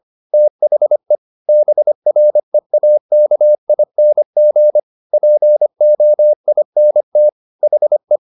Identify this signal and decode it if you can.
Morse code, heard as a keyed tone, about 25 wpm, 600 Hz. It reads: THE BREAKING POINT HE
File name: the old man and the sea part 1 25wpm